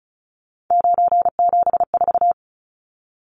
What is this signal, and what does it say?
Morse code, heard as a keyed tone, about 34 wpm, 700 Hz. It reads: 974
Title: Numbers - 3-Digit 35wpm